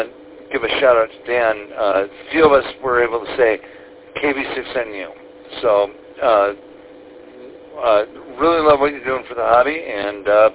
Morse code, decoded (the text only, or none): none